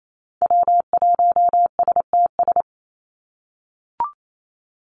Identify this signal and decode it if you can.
Morse code, heard as a keyed tone, about 28 wpm, 700 Hz. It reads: W1STS
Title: USA Call Signs 28wpm